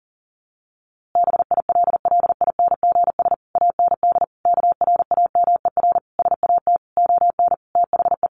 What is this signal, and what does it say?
Morse code, heard as a keyed tone, about 40 wpm, 700 Hz. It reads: BILLINGS AND KRAMER SAT ON THE
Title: A Sound Of Thunder 40wpm